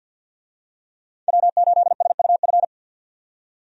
Morse code, heard as a keyed tone, about 51 wpm, 700 Hz. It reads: W8SUF
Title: USA Call Signs - 5 Characters - 50wpm